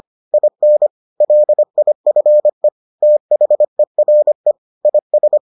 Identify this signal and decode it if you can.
Morse code, heard as a keyed tone, about 25 wpm, 600 Hz. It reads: IN LIFE THERE IS